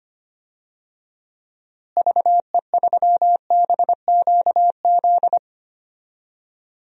Morse code, heard as a keyed tone, about 25 wpm, 700 Hz. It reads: VE3BQZ
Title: DXCC-100 Call Signs - 6 Characters - Repeated 3x - 25wpm